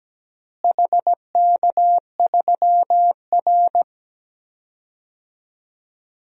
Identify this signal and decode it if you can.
Morse code, heard as a keyed tone, about 17 wpm, 700 Hz. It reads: HK3R
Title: DXCC-100 Call Signs - 4 Characters - Repeated 3x - 17wpm